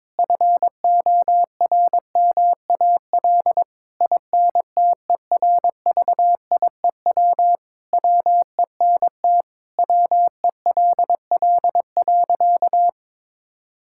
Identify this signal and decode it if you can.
Morse code, heard as a keyed tone, about 22 wpm, 700 Hz. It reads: FORMAL INTERVIEW WENT WELL.
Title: Sentences from Top 1600 Words 22wpm